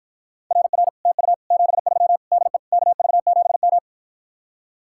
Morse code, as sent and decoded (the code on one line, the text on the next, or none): .-. .-. / - ..- / --... ...-- / -.. / -.- ...- --... --
RR TU 73 DE KV7M